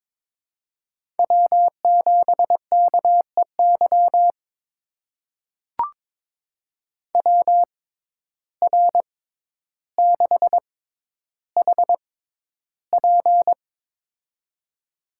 Morse code, heard as a keyed tone, about 22 wpm, spaced as for 11 wpm, 700 Hz. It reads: W7KEY WR6HP
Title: USA Call Signs - 5 Characters - 6x Character Spacing - 22wpm